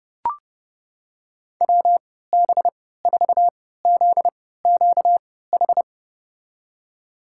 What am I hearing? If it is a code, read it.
Morse code, heard as a keyed tone, about 32 wpm, spaced as for 17 wpm, 700 Hz. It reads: WB4ZQH
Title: USA Call Signs - 6 Characters - 3x Character Spacing - 30wpm